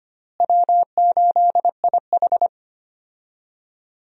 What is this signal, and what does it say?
Morse code, heard as a keyed tone, about 25 wpm, 700 Hz. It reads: W8IH
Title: USA Call Signs - 4 Characters - Repeated 2x - 25wpm